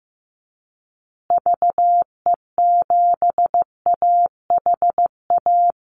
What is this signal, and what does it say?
Morse code, heard as a keyed tone, about 15 wpm, 700 Hz. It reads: VE7AHA
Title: Call Signs Repeated 2x - 15wpm